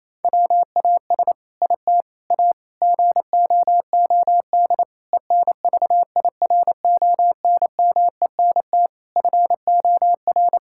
Morse code, heard as a keyed tone, about 28 wpm, 700 Hz. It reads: WAS IT A GOOD ENVIRONMENT FOR
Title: Sentences from Top 1000 Words - Review Part 5 - 28wpm